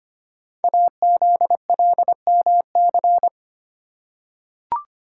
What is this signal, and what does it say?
Morse code, heard as a keyed tone, about 25 wpm, 700 Hz. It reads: AZLMC
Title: Code Groups - 5 Letters - 25wpm